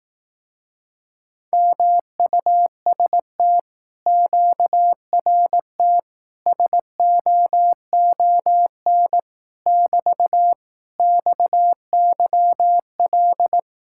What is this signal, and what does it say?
Morse code, heard as a keyed tone, about 18 wpm, 700 Hz. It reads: MUST QRT SOON <BT> XYL
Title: Speed-Racing Common QSO Phrases Encore - 15wpm